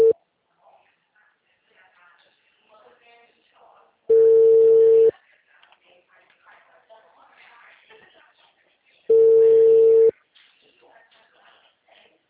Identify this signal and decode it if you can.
Ringback tone (China, 450 Hz), with speech.
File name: out- -2032- - - .77379